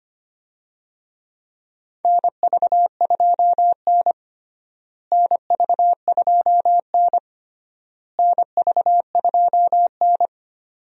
Morse code, heard as a keyed tone, about 25 wpm, 700 Hz. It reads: NV2N NV2N NV2N
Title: USA Call Signs - 4 Characters - Repeated 3x - 25wpm